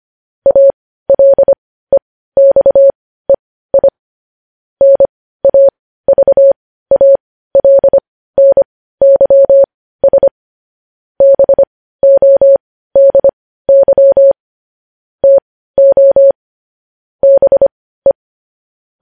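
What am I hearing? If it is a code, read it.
Morse code, heard as a keyed tone, about 25 wpm, spaced as for 15 wpm, 550 Hz. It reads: ALEXEI NAVALNYS BODY TO BE
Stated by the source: Morse code news headlines sent at 15 words per minute.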